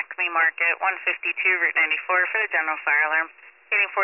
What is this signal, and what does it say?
Emergency Alert System (SAME) tones, heard as harsh buzzing bursts, with speech. It is heard from 0.3 to 3.3 s.